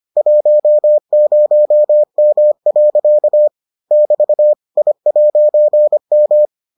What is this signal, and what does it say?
Morse code, heard as a keyed tone, about 25 wpm, 600 Hz. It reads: 10M. <BT> I'M